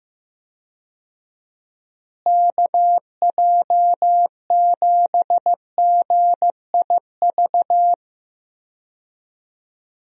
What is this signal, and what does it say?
Morse code, heard as a keyed tone, about 15 wpm, 700 Hz. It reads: KJ7GIV
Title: USA Call Signs - 6 Characters - Repeated 2x - 15wpm